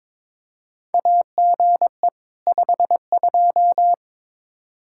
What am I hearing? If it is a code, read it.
Morse code, heard as a keyed tone, about 22 wpm, 700 Hz. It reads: AGE 52
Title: Practice QSO - Exchange 3 - More Info - 22wpm